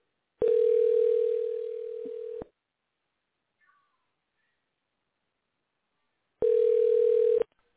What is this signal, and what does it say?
Ringback tone (North America), 440 + 480 Hz.